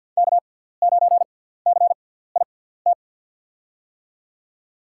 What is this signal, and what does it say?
Morse code, heard as a keyed tone, about 46 wpm, spaced as for 17 wpm, 700 Hz. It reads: K9CIT
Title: USA Call Signs - 5 Characters - 6x Character Spacing - 50wpm